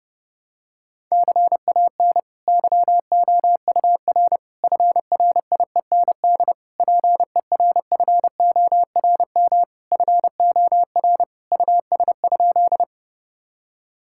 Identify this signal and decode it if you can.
Morse code, heard as a keyed tone, about 30 wpm, 700 Hz. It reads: CAN YOUR FRIEND PERFORM FOR US?
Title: Sentences from Top 1000 Words - Review Part 5 - 30wpm